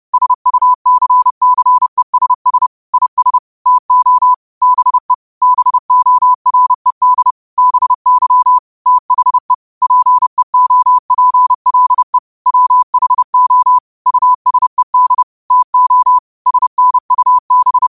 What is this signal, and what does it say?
Morse code, heard as a keyed tone, about 30 wpm, 1000 Hz. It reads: SUCCESS IS TO BE BORED BY THE PEOPLE WHO USED TO SNUB